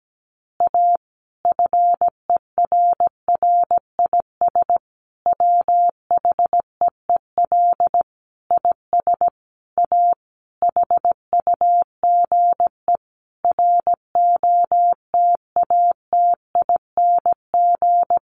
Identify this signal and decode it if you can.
Morse code, heard as a keyed tone, about 17 wpm, 700 Hz. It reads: A FERRIS WHEEL IS A HUGE ROTATING
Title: Additional Easy Phrases 17wpm